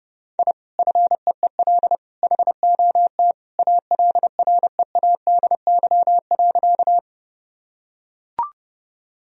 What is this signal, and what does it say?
Morse code, heard as a keyed tone, about 30 wpm, 700 Hz. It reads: I FEEL HOT ALREADY.
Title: Sentences from Top 1000 Words - Review Part 3 - 30wpm